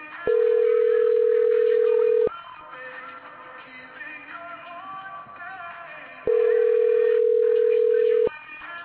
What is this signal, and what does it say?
Ringback tone (North America), 440 + 480 Hz.